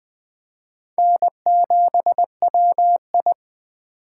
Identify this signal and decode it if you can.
Morse code, heard as a keyed tone, about 19 wpm, 700 Hz. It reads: N7WI